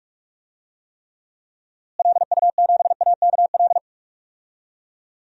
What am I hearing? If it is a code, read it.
Morse code, heard as a keyed tone, about 43 wpm, 700 Hz. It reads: LU7AKL
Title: DXCC-100 Call Signs - 6 Characters - Repeated 3x - 45wpm